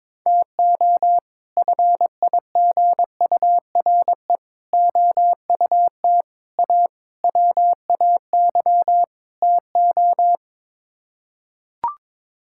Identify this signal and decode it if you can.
Morse code, heard as a keyed tone, about 22 wpm, 700 Hz. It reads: TO FIGURE OUT A WAY TO
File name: practice / 22wpm / Sets of 6 Words - 22wpm